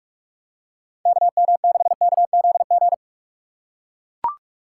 Morse code, heard as a keyed tone, about 43 wpm, 700 Hz. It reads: KM6KZG